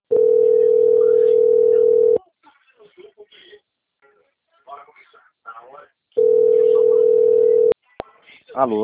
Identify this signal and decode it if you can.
Ringback tone (North America), 440 + 480 Hz, with speech.
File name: out- - - - - .17928